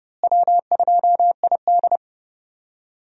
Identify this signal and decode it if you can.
Morse code, heard as a keyed tone, about 30 wpm, 700 Hz. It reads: W2ID
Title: USA Call Signs - 4 Characters - Repeated 3x - 30wpm